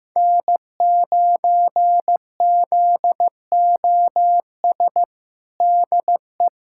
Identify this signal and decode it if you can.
Morse code, heard as a keyed tone, about 15 wpm, 700 Hz. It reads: N9ZOS DE
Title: USA Callsign DE Callsign - 5 Characters - 15wpm